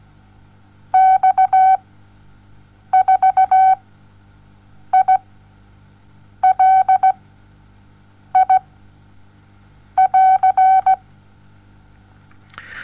Morse code, heard as a keyed tone, about 17 wpm, spaced as for 6 wpm, 760 Hz. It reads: X4ILI<AR>
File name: Onlinelehrgang / morsen / DJ4UF Morsen07